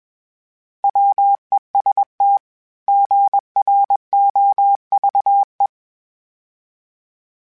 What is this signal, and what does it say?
Morse code, heard as a keyed tone, about 21 wpm, 800 Hz. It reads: WEST GROVE
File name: fromSMs / FromSMs mp3s / Cities@20